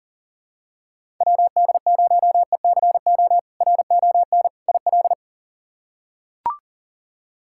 Morse code, heard as a keyed tone, about 40 wpm, 700 Hz. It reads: WD0ECO RON IL
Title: K1USN SST QSO Practice - 40wpm